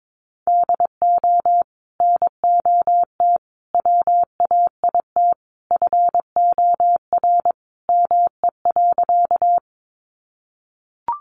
Morse code, heard as a keyed tone, about 22 wpm, 700 Hz. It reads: DO NOT WAIT FOR ME.
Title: Sentences from Top 600 Words 22wpm